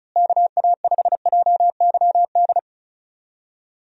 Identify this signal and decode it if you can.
Morse code, heard as a keyed tone, about 34 wpm, 700 Hz. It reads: KA5JYD
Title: USA Call Signs - 6 Characters - Repeated 2x - 35wpm